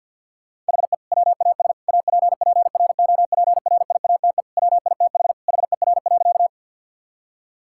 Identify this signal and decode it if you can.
Morse code, heard as a keyed tone, about 47 wpm, 700 Hz. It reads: HE WAS APPROPRIATE WITH HER.
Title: Sentences from Top 1000 Words - Review Part 5 - 50wpm